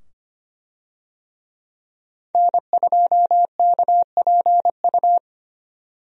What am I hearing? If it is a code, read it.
Morse code, heard as a keyed tone, about 25 wpm, 700 Hz. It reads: N2KPU
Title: USA Call Signs - 5 Characters - 25wpm